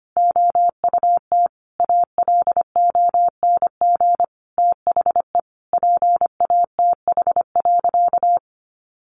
Morse code, heard as a keyed tone, about 25 wpm, 700 Hz. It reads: OUT ALONG THE PATH.
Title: A Sound Of Thunder 25wpm